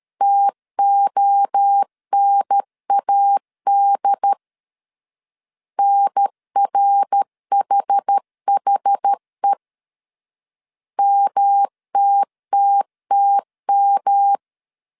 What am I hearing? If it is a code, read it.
Morse code, heard as a keyed tone, about 13 wpm, 800 Hz. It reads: TONAD NRHHE MTTTM
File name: CW / DET / 1 3 1500C S C 12wpm 2750